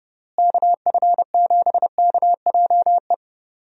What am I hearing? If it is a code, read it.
Morse code, heard as a keyed tone, about 32 wpm, 700 Hz. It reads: KF7KJE
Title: USA Call Signs - 6 Characters - Repeated 2x - 30wpm